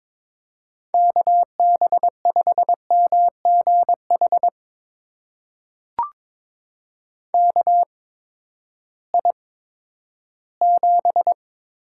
Morse code, heard as a keyed tone, about 22 wpm, spaced as for 13 wpm, 700 Hz. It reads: KB5MGH KI7
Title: USA Call Signs - 6 Characters - 8x Character Spacing - 22wpm